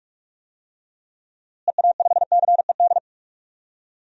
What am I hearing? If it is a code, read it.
Morse code, heard as a keyed tone, about 46 wpm, 700 Hz. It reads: EA5CED
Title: DXCC-100 Call Signs - 6 Characters - Repeated 3x - 45wpm